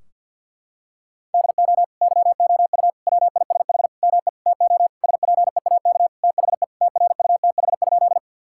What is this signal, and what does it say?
Morse code, heard as a keyed tone, about 50 wpm, 700 Hz. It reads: DO YOU WISH ME TO SPEAK THE TRUTH?
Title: Sentences from Top 1600 Words 50wpm